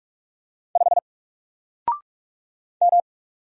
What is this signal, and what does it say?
Morse code, heard as a keyed tone, about 46 wpm, 700 Hz. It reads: 5 M
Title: Single Letter-Number 45wpm